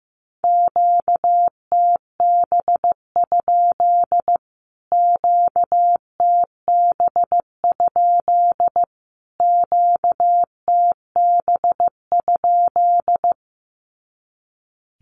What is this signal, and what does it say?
Morse code, heard as a keyed tone, about 15 wpm, 700 Hz. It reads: QTB? QTB? QTB?
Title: Q-Codes Repeated 3x 15wpm